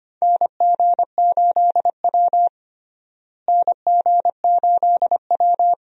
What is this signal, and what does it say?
Morse code, heard as a keyed tone, about 25 wpm, 700 Hz. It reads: NG8W NG8W
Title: USA Call Signs - 4 Characters - Repeated 2x - 25wpm